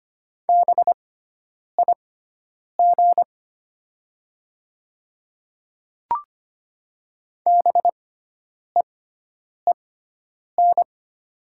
Morse code, heard as a keyed tone, about 25 wpm, spaced as for 9 wpm, 700 Hz. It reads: BIG BEEN
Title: 6x Character Spacing - Top 200 Words - 25wpm